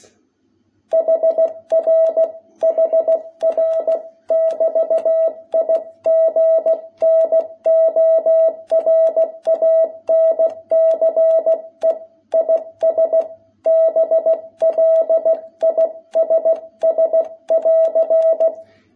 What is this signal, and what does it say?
Morse code, heard as a keyed tone, about 16 wpm, 660 Hz. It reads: HR HR<BT>IGNORANCE IS BLISS<AR>
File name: jh8cbh / 2021cw